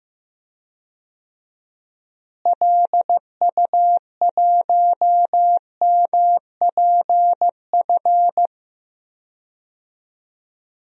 Morse code, heard as a keyed tone, about 15 wpm, 700 Hz. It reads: LU1MPF
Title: DXCC-100 Call Signs - 6 Characters - Repeated 2x - 15wpm